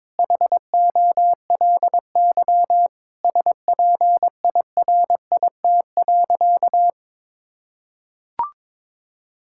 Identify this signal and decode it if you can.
Morse code, heard as a keyed tone, about 22 wpm, 700 Hz. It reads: HOLY SPIRIT.
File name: Sentences with 9-Word Phrases - 22wpm